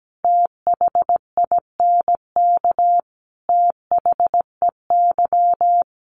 Morse code, heard as a keyed tone, about 17 wpm, 700 Hz. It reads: THINK THEY
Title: Sentences from Top 600 Words 17wpm